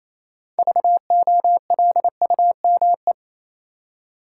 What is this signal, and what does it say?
Morse code, heard as a keyed tone, about 28 wpm, 700 Hz. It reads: VOLUME
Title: Six Letter Words - 28wpm